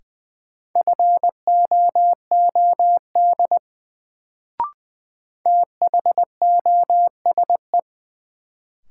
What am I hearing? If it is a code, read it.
Morse code, heard as a keyed tone, about 20 wpm, 700 Hz. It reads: FOOD THOSE
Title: Sentences from Top 300 Words - New Word Drill - 20wpm